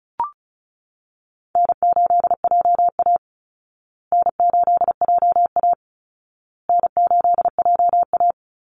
Morse code, heard as a keyed tone, about 34 wpm, 700 Hz. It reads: N8JA N8JA N8JA
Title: Call Signs Repeated 2x - 35wpm